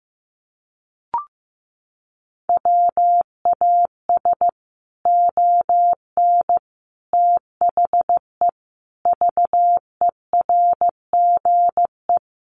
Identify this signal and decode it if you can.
Morse code, heard as a keyed tone, about 15 wpm, 700 Hz. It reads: WAS ON THE VERGE
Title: Sets of 5 Words Encore 15wpm